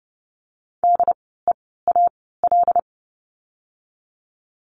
Morse code, heard as a keyed tone, about 30 wpm, 700 Hz. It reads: D E A L
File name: Lesson 16f - Letter c - Words 30wpm